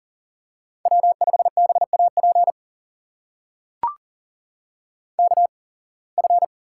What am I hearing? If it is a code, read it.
Morse code, heard as a keyed tone, about 38 wpm, spaced as for 26 wpm, 700 Hz. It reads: W5BAP KF
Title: USA Call Signs - 5 Characters - 8x Character Spacing - 40wpm